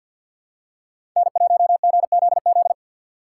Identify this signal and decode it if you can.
Morse code, heard as a keyed tone, about 51 wpm, 700 Hz. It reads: N1GZZ
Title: USA Call Signs - 5 Characters - 4x Character Spacing - 50wpm